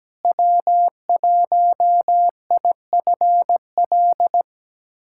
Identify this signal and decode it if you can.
Morse code, heard as a keyed tone, about 17 wpm, 700 Hz. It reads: W1IFL
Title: USA Callsign DE Callsign - 5 Characters - 17wpm